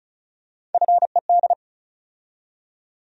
Morse code, heard as a keyed tone, about 36 wpm, 700 Hz. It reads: FED
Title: USA Call Sign Postfixes - 3 Letters 35wpm